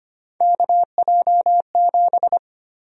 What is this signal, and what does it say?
Morse code, heard as a keyed tone, about 25 wpm, 700 Hz. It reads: KJ7
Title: USA Call Sign Prefixes - Repeated 2x - 25wpm